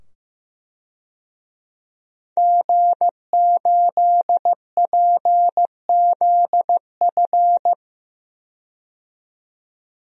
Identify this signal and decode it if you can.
Morse code, heard as a keyed tone, about 16 wpm, 700 Hz. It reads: G8PZF
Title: DXCC-100 Call Signs - 5 Characters - 15wpm